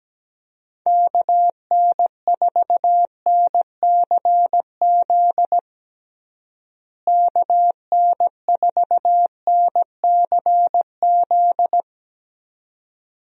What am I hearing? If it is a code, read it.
Morse code, heard as a keyed tone, about 17 wpm, 700 Hz. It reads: KN4NCZ KN4NCZ
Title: USA Call Signs - 6 Characters - Repeated 3x - 17wpm